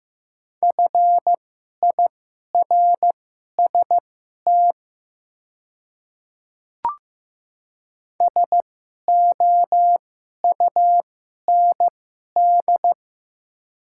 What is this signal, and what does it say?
Morse code, heard as a keyed tone, about 15 wpm, spaced as for 11 wpm, 700 Hz. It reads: FIRST SOUND
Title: 2x Character Spacing - Top 100 Words - 15wpm